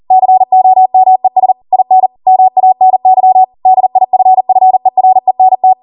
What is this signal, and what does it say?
Morse code, heard as a keyed tone, about 40 wpm, 750 Hz. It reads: COMES IN MANY DIFFERENT